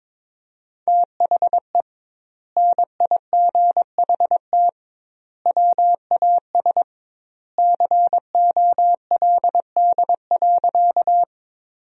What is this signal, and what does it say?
Morse code, heard as a keyed tone, about 22 wpm, 700 Hz. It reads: THE NIGHT WAS COLD.
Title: 2x Word-Spacing - Sentences from Top 400 Words - 22wpm